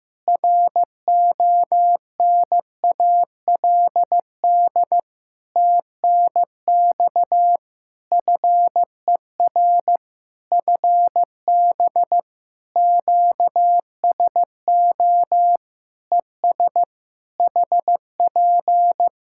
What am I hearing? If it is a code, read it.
Morse code, heard as a keyed tone, about 15 wpm, 700 Hz. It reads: RONALD TNX FER FB QSO ES HP
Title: Practice QSO - Exchange 4 - Ending - 15wpm